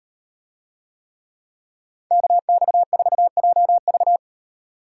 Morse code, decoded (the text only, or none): KX4JV